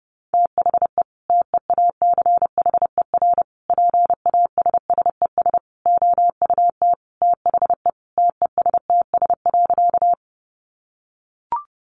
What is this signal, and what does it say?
Morse code, heard as a keyed tone, about 28 wpm, 700 Hz. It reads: THE TEACHER PASSES OUT THE TESTS.
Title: Additional Easy Phrases 30wpm